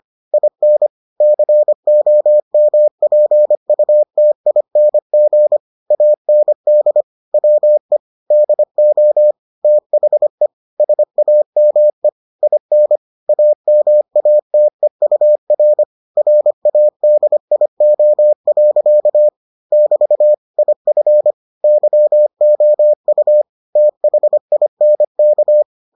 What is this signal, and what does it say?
Morse code, heard as a keyed tone, about 25 wpm, 600 Hz. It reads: IN COMPUTING AND WE DO THE SAME IN AMATEUR RADIO. <BT> IF YOU THINK